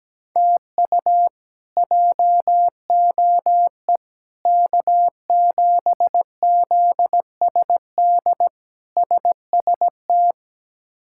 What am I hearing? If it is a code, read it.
Morse code, heard as a keyed tone, about 17 wpm, 700 Hz. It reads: TU JOE K7ZSD SST
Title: K1USN SST QSO Practice - 17wpm